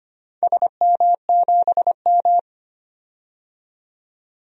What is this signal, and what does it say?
Morse code, heard as a keyed tone, about 25 wpm, 700 Hz. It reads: SM7M
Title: DXCC-100 Call Signs - 4 Characters - 25wpm